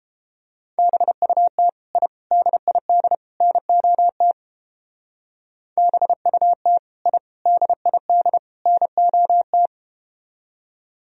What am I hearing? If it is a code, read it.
Morse code, heard as a keyed tone, about 31 wpm, 700 Hz. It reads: BUT I DID NOT BUT I DID NOT